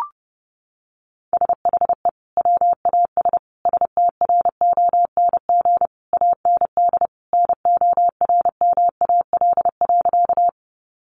Morse code, heard as a keyed tone, about 28 wpm, 700 Hz. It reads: SHE WAS STRONG AND NORMAL.
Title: Sentences from Top 1200 Words 30wpm